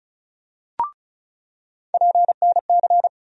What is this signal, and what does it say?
Morse code, heard as a keyed tone, about 36 wpm, 700 Hz. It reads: PNC
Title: USA Call Sign Postfixes - 3 Letters 35wpm